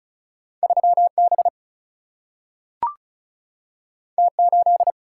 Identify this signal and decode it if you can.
Morse code, heard as a keyed tone, about 36 wpm, 700 Hz. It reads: 3B T8
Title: Single-Letter Single-Number 35wpm